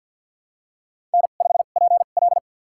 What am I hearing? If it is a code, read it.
Morse code, heard as a keyed tone, about 46 wpm, spaced as for 35 wpm, 700 Hz. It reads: N5PL
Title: USA Call Signs - 4 Characters - 2x Character Spacing - 50wpm